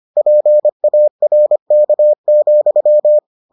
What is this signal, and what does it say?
Morse code, heard as a keyed tone, about 25 wpm, 600 Hz. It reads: PARK,